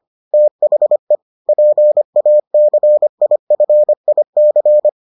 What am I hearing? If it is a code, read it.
Morse code, heard as a keyed tone, about 25 wpm, 600 Hz. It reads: THE PACIFIC